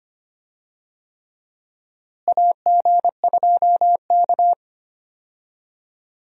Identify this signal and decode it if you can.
Morse code, heard as a keyed tone, about 25 wpm, 700 Hz. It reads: AG2K
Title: USA Call Signs - 4 Characters - Repeated 3x - 25wpm